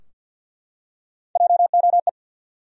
Morse code, heard as a keyed tone, about 53 wpm, 700 Hz. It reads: JOE
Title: USA Common Names - Max 4 Letters - 50wpm